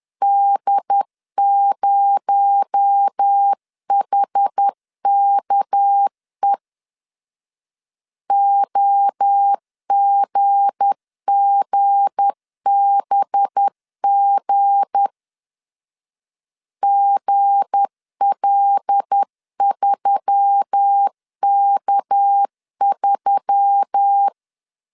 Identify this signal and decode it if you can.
Morse code, heard as a keyed tone, about 10 wpm, 800 Hz. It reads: D0HKE OGGBG GL3K3
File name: LEZIONI / 7 GF9Z / 7 G 10wpm 2501